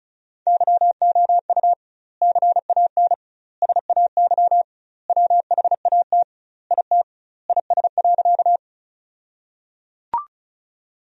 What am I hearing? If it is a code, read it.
Morse code, heard as a keyed tone, about 36 wpm, 700 Hz. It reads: YOU CAN SAY WHAT IT IS.
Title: 2x Word-Spacing - Sentences from Top 200 Words - 35wpm